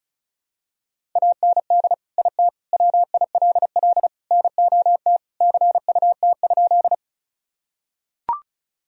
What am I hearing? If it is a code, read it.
Morse code, heard as a keyed tone, about 36 wpm, 700 Hz. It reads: AND IT WILL NOT CUT?
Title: Sentences from Top 1000 Words - Review Part 1 - 35wpm